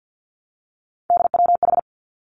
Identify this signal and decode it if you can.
Morse code, heard as a keyed tone, about 50 wpm, 700 Hz. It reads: DWH
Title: Three Challenging Letter-Number Combinations 50wpm